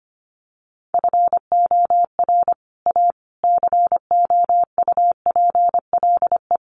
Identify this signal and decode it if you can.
Morse code, heard as a keyed tone, about 25 wpm, 700 Hz. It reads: FOR A COUPLE
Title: Sets of 5 Words Encore2 25wpm